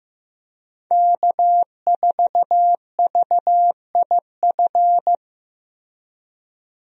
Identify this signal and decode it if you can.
Morse code, heard as a keyed tone, about 15 wpm, 700 Hz. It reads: K4VIF
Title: USA Call Signs - 5 Characters - Repeated 2x - 15wpm